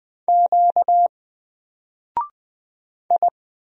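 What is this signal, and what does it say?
Morse code, heard as a keyed tone, about 20 wpm, 700 Hz. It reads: Q I
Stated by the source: Single letter and number Morse drill at 20 wpm, from N4LCD.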